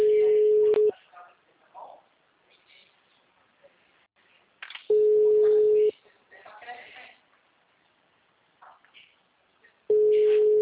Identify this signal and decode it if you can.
Ringback tone (Europe, 425 Hz), with speech.
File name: out- -2051- - - .75754